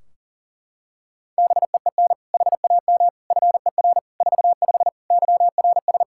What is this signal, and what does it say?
Morse code, heard as a keyed tone, about 40 wpm, 700 Hz. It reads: BEEN HAM FER 45 YRS
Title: K1USN SST QSO Practice 40wpm